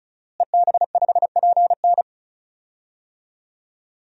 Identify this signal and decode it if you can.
Morse code, heard as a keyed tone, about 36 wpm, 700 Hz. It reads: EB5PN
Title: DXCC-100 Call Signs - 5 Characters - Repeated 2x - 35wpm